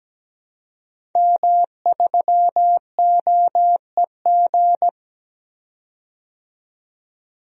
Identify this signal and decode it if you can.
Morse code, heard as a keyed tone, about 17 wpm, 700 Hz. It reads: M3OEG
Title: DXCC-100 Call Signs - 5 Characters - 17wpm